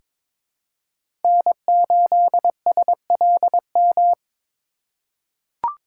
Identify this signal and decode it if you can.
Morse code, heard as a keyed tone, about 22 wpm, 700 Hz. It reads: N8SLM
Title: USA Call Signs - 3x Character Spacing - 22wpm